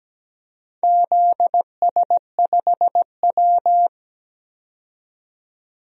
Morse code, heard as a keyed tone, about 17 wpm, 700 Hz. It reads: ZS5W
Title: DXCC-100 Call Signs - 4 Characters - 17wpm